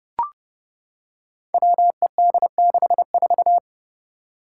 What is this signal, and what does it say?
Morse code, heard as a keyed tone, about 32 wpm, 700 Hz. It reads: WED64
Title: Five Challenging Letter-Number Combinations 30wpm